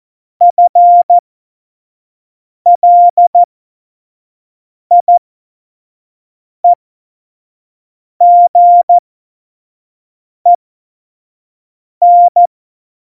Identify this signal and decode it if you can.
Morse code, heard as a keyed tone, about 14 wpm, spaced as for 5 wpm, 700 Hz. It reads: FLIEGEN
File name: Morsekurs / Witze / Witz 22. August 17, Tempo 25 F70